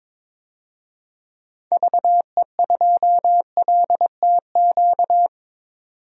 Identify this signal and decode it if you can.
Morse code, heard as a keyed tone, about 21 wpm, 700 Hz. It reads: VE2LTQ
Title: DXCC-100 Call Signs - 6 Characters - 22wpm